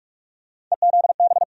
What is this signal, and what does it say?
Morse code, heard as a keyed tone, about 47 wpm, 700 Hz. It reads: EZB
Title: USA Call Sign Postfixes - 3 Letters 45wpm